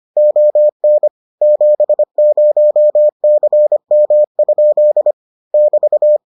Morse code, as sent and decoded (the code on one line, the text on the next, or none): --- -. / --... ----- -.-. -- ..--.. / -...-
ON 70CM? <BT>